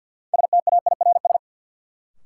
Morse code, heard as a keyed tone, about 53 wpm, 700 Hz. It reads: STAIRS
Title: Sets of 5 Words Encore - New Word Drill - 50wpm